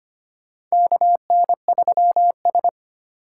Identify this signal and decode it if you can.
Morse code, heard as a keyed tone, about 25 wpm, 700 Hz. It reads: KN3S
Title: USA Call Signs - 2x Character Spacing - 25wpm